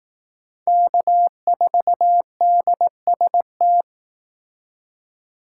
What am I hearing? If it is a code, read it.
Morse code, heard as a keyed tone, about 20 wpm, 700 Hz. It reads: K4DST